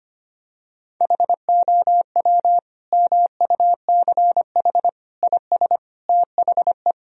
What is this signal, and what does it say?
Morse code, heard as a keyed tone, about 25 wpm, 700 Hz. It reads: HOW MUCH IS THE
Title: Sentences from Top 1000 Words - Review Part 1 - 25wpm